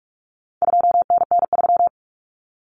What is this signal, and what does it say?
Morse code, heard as a keyed tone, about 44 wpm, 700 Hz. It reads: 2NN3
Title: Common Cut Numbers - 4-Digit 45wpm